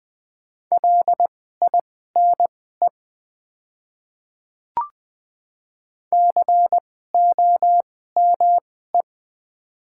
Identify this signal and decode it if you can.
Morse code, heard as a keyed tone, about 20 wpm, spaced as for 14 wpm, 690 Hz. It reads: LINE COME
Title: 2x Character Spacing - Top 500 Words - 20wpm